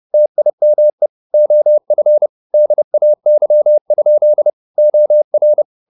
Morse code, heard as a keyed tone, about 30 wpm, 600 Hz. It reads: TIME OF DAY? OR
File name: the old man and the sea / the old man and the sea 1 30wpm